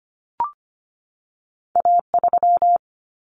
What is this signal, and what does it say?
Morse code, heard as a keyed tone, about 25 wpm, 700 Hz. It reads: A 3
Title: Common Cut Numbers - 2-Digit 25wpm